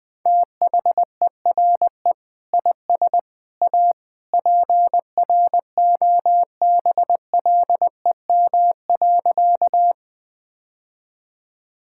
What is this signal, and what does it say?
Morse code, heard as a keyed tone, about 21 wpm, 700 Hz. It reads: THERE IS A PROBLEM.
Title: Sentences with 3-Word Phrases - 20wpm